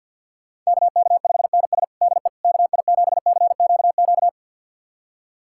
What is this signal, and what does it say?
Morse code, heard as a keyed tone, about 52 wpm, 700 Hz. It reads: KK5NS DE KI7CQQ
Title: USA Callsign DE Callsign 50wpm